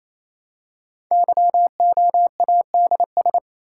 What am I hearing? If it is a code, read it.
Morse code, heard as a keyed tone, about 28 wpm, 700 Hz. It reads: YOADS
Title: Code Groups - 5 Letter:Numbers - 28wpm